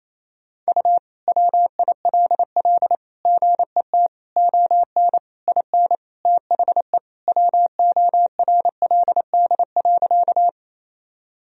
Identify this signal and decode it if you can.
Morse code, heard as a keyed tone, about 28 wpm, 700 Hz. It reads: U WILL GET ON IN THE WORLD.
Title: Sentences from Top 200 Words and Abbreviations 28wpm